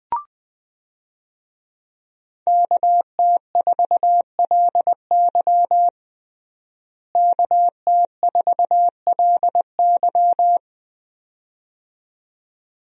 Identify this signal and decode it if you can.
Morse code, heard as a keyed tone, about 20 wpm, 700 Hz. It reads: KT4LY KT4LY
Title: USA Call Signs - 5 Characters - Repeated 2x - 20wpm